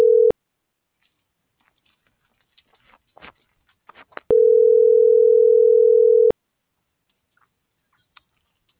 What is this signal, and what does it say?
Ringback tone (North America), 440 + 480 Hz.